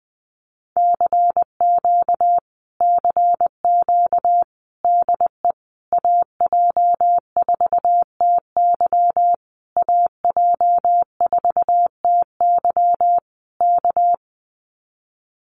Morse code, heard as a keyed tone, about 20 wpm, 700 Hz. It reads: CQ CQ DE AJ4TY AJ4TY K
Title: Common QSO Phrases Encore 20wpm